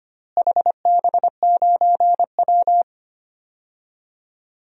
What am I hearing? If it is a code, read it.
Morse code, heard as a keyed tone, about 25 wpm, 700 Hz. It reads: HB9W
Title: DXCC-100 Call Signs - 4 Characters - Repeated 2x - 25wpm